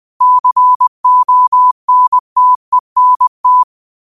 Morse code, heard as a keyed tone, about 20 wpm, 1000 Hz. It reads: CONTENT
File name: qotdfiles / 20wpm